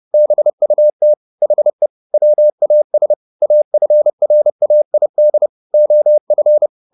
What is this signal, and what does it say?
Morse code, heard as a keyed tone, about 30 wpm, 600 Hz. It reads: BUT HE WAS AFRAID OF
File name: the old man and the sea / the old man and the sea 1 30wpm